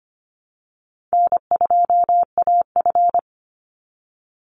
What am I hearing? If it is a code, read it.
Morse code, heard as a keyed tone, about 25 wpm, 700 Hz. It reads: N2AF
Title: NA SOTA Chaser Callsigns 25wpm